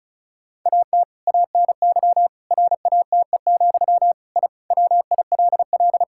Morse code, heard as a keyed tone, about 36 wpm, 700 Hz. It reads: AT ANY RATE, I WILL
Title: Sentences from Top 1000 Words - Review Part 6 - 35wpm